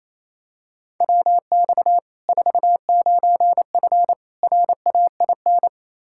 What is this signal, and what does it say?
Morse code, heard as a keyed tone, about 28 wpm, 700 Hz. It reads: WX 49F RAIN
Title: Practice QSO - Exchange 2 - Info - 28wpm